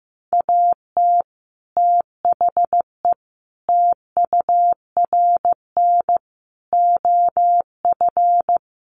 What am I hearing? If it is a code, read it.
Morse code, heard as a keyed tone, about 15 wpm, 700 Hz. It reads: AT THE TURN OF
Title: Sets of 5 Words 15wpm